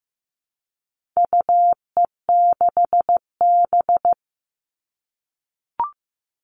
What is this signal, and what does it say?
Morse code, heard as a keyed tone, about 15 wpm, 700 Hz. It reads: UE6B
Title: All Cut Numbers - 4-Digit 15wpm